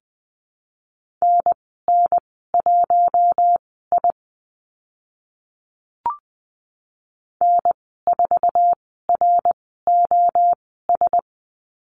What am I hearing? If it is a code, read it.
Morse code, heard as a keyed tone, about 20 wpm, spaced as for 14 wpm, 700 Hz. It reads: NN1I N4ROS
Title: 069 Lesson 12g - Letter d - Call Signs 20wpm